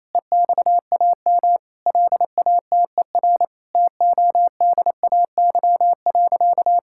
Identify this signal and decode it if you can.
Morse code, heard as a keyed tone, about 28 wpm, 700 Hz. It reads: EXAM LATER TODAY.
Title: Sentences from Top 1000 Words - Review Part 5 - 28wpm